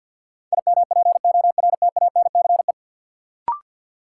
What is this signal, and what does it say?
Morse code, heard as a keyed tone, about 53 wpm, 700 Hz. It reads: IMPORTANCE
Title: Sentences from Top 1200 Words - New Word Drill - 50wpm